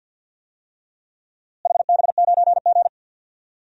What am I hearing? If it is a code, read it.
Morse code, heard as a keyed tone, about 53 wpm, 700 Hz. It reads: HB9G
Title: DXCC-100 Call Signs - 4 Characters - Repeated 3x - 50wpm